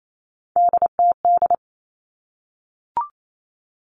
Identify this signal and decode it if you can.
Morse code, heard as a keyed tone, about 28 wpm, 700 Hz. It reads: DTD
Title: Three Challenging Letter-Number Combinations 28wpm